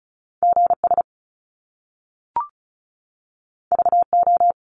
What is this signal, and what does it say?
Morse code, heard as a keyed tone, about 36 wpm, 700 Hz. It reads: GS VO
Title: Top Two Letter Combinations in Words 35wpm